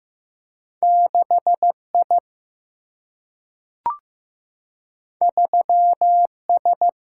Morse code, heard as a keyed tone, about 15 wpm, 700 Hz. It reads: 6I 3S
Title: Single-Letter Single-Number 15wpm